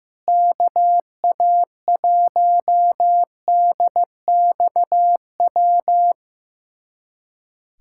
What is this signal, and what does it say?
Morse code, heard as a keyed tone, about 16 wpm, 700 Hz. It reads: KA1DXW
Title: USA Callsign DE Callsign - 6 Characters - 15wpm